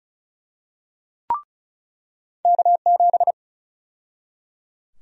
Morse code, heard as a keyed tone, about 34 wpm, 700 Hz. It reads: K7
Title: USA Call Sign Prefixes - 35wpm